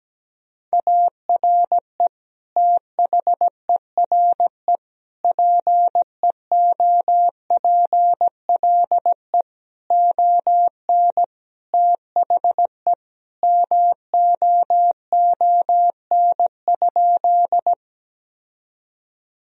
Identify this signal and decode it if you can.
Morse code, heard as a keyed tone, about 17 wpm, 700 Hz. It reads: ARE THERE PEOPLE ON THE MOON?
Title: Sentences from Top 1000 Words - Review Part 3 - 17wpm